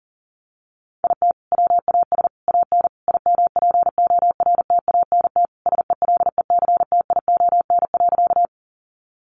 Morse code, heard as a keyed tone, about 37 wpm, 700 Hz. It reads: IT WAS AN IMPORTANT SELECTION.
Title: Sentences from Top 1400 Words 40wpm